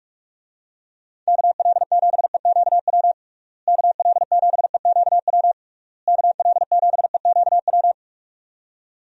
Morse code, heard as a keyed tone, about 41 wpm, 700 Hz. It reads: KL7EQW KL7EQW KL7EQW
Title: DXCC-100 Call Signs - 6 Characters - Repeated 3x - 45wpm